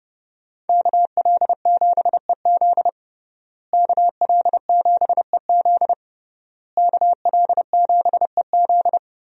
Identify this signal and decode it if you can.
Morse code, heard as a keyed tone, about 31 wpm, 700 Hz. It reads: KL7EZ KL7EZ KL7EZ
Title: DXCC-100 Call Signs - 5 Characters - Repeated 3x - 30wpm